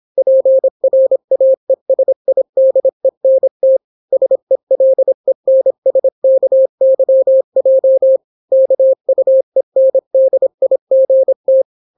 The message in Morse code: PRAESIDENT SELENSKYJ KUENDIGT